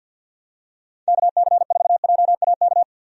Morse code, heard as a keyed tone, about 50 wpm, 700 Hz. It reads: KC4JAK
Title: USA Call Signs - 6 Characters - 50wpm